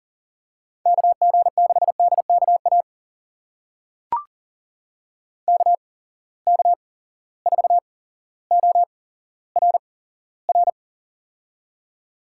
Morse code, heard as a keyed tone, about 42 wpm, spaced as for 17 wpm, 700 Hz. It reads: KG6DKA KK4ORR
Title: USA Call Signs - 6 Characters - 8x Character Spacing - 40wpm